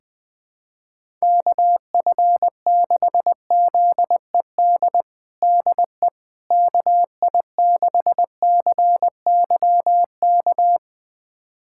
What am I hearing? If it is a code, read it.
Morse code, heard as a keyed tone, about 20 wpm, 700 Hz. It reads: KF6ZED DE KI6CYK
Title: USA Callsign DE Callsign - 6 Characters - 20wpm